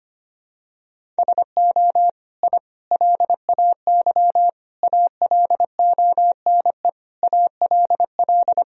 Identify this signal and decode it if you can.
Morse code, heard as a keyed tone, about 25 wpm, 700 Hz. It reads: SO I LAY ALONE ALL
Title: Sentences from Top 1000 Words - Review Part 2 - 25wpm